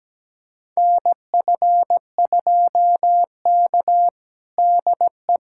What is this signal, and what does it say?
Morse code, heard as a keyed tone, about 17 wpm, 700 Hz. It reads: NF2K DE
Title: USA Callsign DE Callsign - 4 Characters - 17wpm